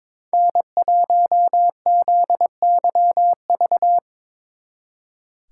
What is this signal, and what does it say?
Morse code, heard as a keyed tone, about 25 wpm, 700 Hz. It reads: N1ZYV
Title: Speed Racing - USA Call Signs 22wpm